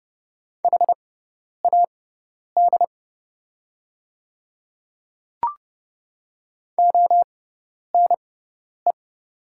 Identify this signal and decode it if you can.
Morse code, heard as a keyed tone, about 31 wpm, spaced as for 10 wpm, 700 Hz. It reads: HAD ONE
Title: 6x Character Spacing - Top 100 Words - 30wpm